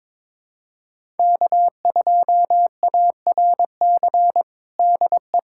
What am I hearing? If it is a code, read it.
Morse code, heard as a keyed tone, about 22 wpm, 700 Hz. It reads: K2ARC DE
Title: USA Callsign DE Callsign - 5 Characters - 22wpm